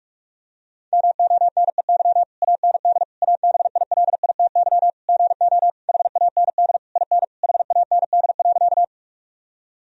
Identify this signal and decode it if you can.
Morse code, heard as a keyed tone, about 46 wpm, 700 Hz. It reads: MONEY AND ABILITY GO HAND IN HAND.
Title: Sentences from Top 1000 Words - Review Part 3 - 45wpm